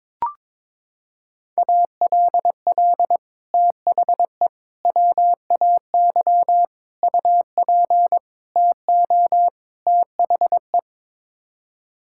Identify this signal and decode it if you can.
Morse code, heard as a keyed tone, about 22 wpm, 700 Hz. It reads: ALL THE WAY UP TO THE
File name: practice / 22wpm / Sets of 6 Words - 22wpm